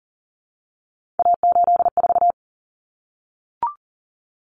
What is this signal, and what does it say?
Morse code, heard as a keyed tone, about 40 wpm, 700 Hz. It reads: A84
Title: Common Cut Numbers - 3-Digit 40wpm